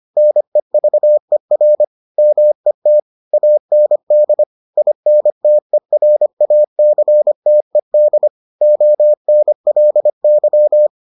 Morse code, heard as a keyed tone, about 25 wpm, 600 Hz. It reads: NEVER MET AND INTERACTED ONLY